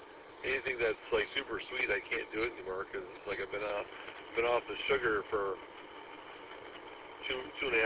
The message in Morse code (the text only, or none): none